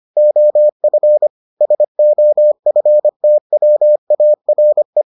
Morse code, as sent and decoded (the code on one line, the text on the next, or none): --- ..-. / ... --- ..-. - .-- .- .-. .
OF SOFTWARE